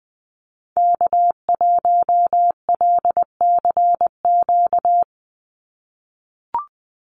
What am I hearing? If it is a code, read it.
Morse code, heard as a keyed tone, about 20 wpm, 700 Hz. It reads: K1LCQ
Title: USA Call Signs 20wpm